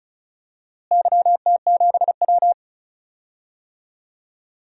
Morse code, heard as a keyed tone, about 36 wpm, 700 Hz. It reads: YT7W
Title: DXCC-100 Call Signs - 4 Characters - Repeated 3x - 35wpm